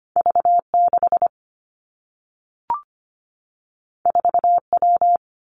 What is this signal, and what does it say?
Morse code, heard as a keyed tone, about 25 wpm, 700 Hz. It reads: V 6 4 W
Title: Two Challenging Letter-Number Combinations 25wpm